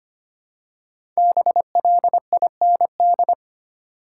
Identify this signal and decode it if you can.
Morse code, heard as a keyed tone, about 25 wpm, 700 Hz. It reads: BLIND
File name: 5 Letter Words - First 3 Letters Given - 25wpm